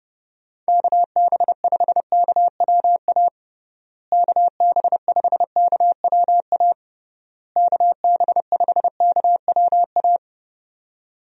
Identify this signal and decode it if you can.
Morse code, heard as a keyed tone, about 29 wpm, 700 Hz. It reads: KB5KWA KB5KWA KB5KWA
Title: USA Call Signs - 6 Characters - Repeated 3x - 30wpm